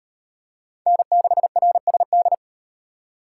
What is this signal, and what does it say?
Morse code, heard as a keyed tone, about 34 wpm, 700 Hz. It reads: N6RSD